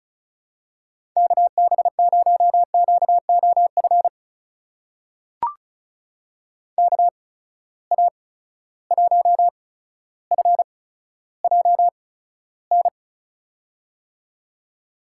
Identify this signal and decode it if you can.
Morse code, heard as a keyed tone, about 36 wpm, spaced as for 15 wpm, 700 Hz. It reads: KB0QOF KA1FJN
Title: USA Call Signs - 6 Characters - 8x Character Spacing - 35wpm